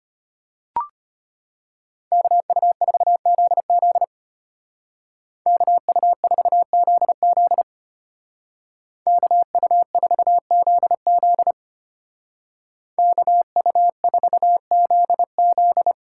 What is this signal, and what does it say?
Morse code, heard as a keyed tone, about 31 wpm, 700 Hz. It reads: KU4ZZ KU4ZZ KU4ZZ KU4ZZ
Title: Speed Racing - USA Call Signs 25wpm